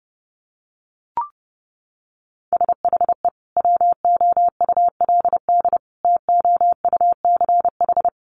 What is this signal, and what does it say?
Morse code, heard as a keyed tone, about 30 wpm, 700 Hz. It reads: SHE WOULD TOUCH
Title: Sentences from Top 1200 Words 30wpm